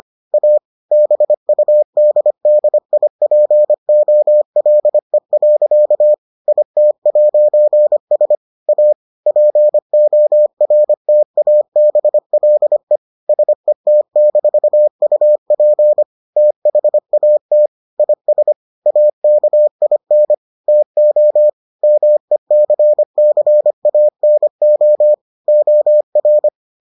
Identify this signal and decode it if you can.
Morse code, heard as a keyed tone, about 25 wpm, 600 Hz. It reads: A BUDDIPOLE. IT'S A PORTABLE SET-UP THAT IS AKIN TO MECCANO OR